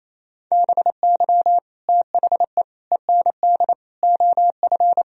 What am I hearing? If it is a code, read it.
Morse code, heard as a keyed tone, about 37 wpm, 700 Hz. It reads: BY THE END OF